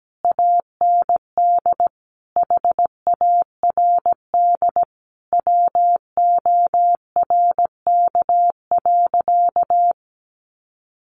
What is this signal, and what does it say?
Morse code, heard as a keyed tone, about 17 wpm, 700 Hz. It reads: AND HARD WORK.
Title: Sentences from Top 1300 Words 17wpm